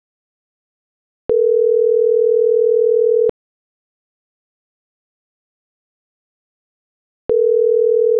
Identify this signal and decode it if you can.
Ringback tone (North America), 440 + 480 Hz.